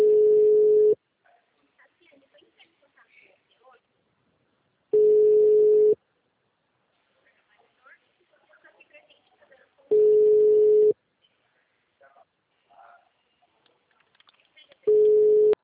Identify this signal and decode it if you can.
Ringback tone (Europe, 425 Hz), with speech.
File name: out- -2011- - - .9205